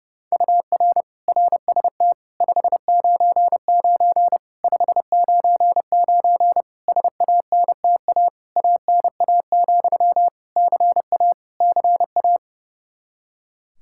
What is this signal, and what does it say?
Morse code, heard as a keyed tone, about 30 wpm, 700 Hz. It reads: UR RST 599 599 SANTA ANA, CA CA
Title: Practice QSO - Exchange 1 - Beginning - 30wpm